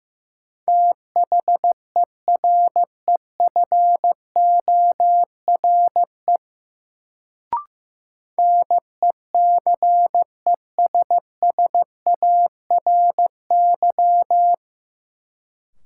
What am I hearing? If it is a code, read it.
Morse code, heard as a keyed tone, about 15 wpm, 700 Hz. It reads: THEREFORE NECESSARY
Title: Sentences from Top 800 Words - New Word Drill - 15wpm